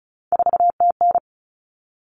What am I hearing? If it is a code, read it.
Morse code, heard as a keyed tone, about 34 wpm, 700 Hz. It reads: 4TN
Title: Common Cut Numbers - 3-Digit 35wpm